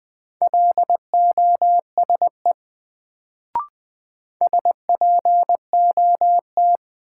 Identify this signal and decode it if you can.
Morse code, heard as a keyed tone, about 20 wpm, 700 Hz. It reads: LOSE SPOT
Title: Sentences from Top 1100 Words - New Word Drill Alt - 20wpm